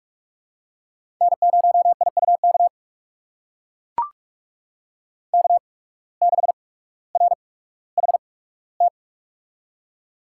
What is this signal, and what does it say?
Morse code, heard as a keyed tone, about 44 wpm, spaced as for 19 wpm, 700 Hz. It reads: N0IUK K6RHT
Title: USA Call Signs - 5 Characters - 8x Character Spacing - 45wpm